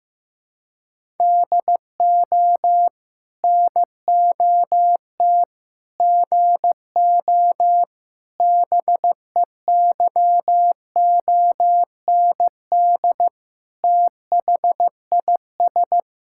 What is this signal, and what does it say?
Morse code, heard as a keyed tone, about 15 wpm, 700 Hz. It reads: DO NOT GO BEYOND THIS
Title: Sentences from Top 1000 Words - Review Part 6 - 15wpm